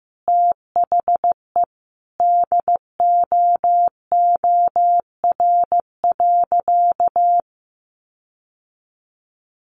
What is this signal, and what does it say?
Morse code, heard as a keyed tone, about 14 wpm, 700 Hz. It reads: THE DOOR.
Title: Sentences from Top 1400 Words 15wpm